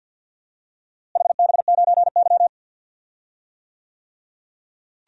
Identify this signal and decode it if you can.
Morse code, heard as a keyed tone, about 46 wpm, 700 Hz. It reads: HB9Y
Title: DXCC-100 Call Signs - 4 Characters - Repeated 3x - 50wpm